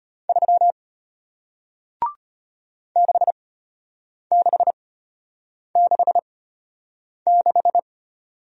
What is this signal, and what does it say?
Morse code, heard as a keyed tone, about 33 wpm, spaced as for 8 wpm, 700 Hz. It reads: 3 6666